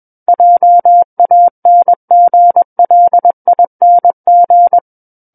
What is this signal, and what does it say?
Morse code, heard as a keyed tone, about 21 wpm, 700 Hz. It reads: JANGLING
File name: Book One / 12-Chapter 12 - 21 WPM